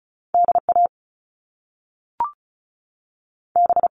Morse code, heard as a keyed tone, about 36 wpm, 700 Hz. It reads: D A B
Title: Top Two Letter Combinations in Words 35wpm